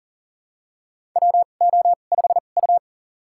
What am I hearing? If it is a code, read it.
Morse code, heard as a keyed tone, about 43 wpm, spaced as for 30 wpm, 690 Hz. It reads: WO5U
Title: USA Call Signs - 4 Characters - 2x Character Spacing - 40wpm